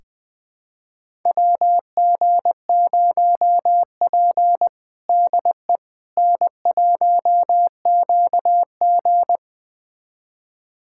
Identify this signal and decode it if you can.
Morse code, heard as a keyed tone, about 20 wpm, 700 Hz. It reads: WG0P DE N1QG
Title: USA Callsign DE Callsign - 4 Characters - 20wpm